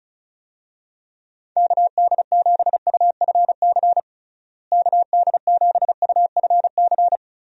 Morse code, heard as a keyed tone, about 36 wpm, 700 Hz. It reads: KD7UFC KD7UFC
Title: USA Call Signs - 6 Characters - Repeated 3x - 35wpm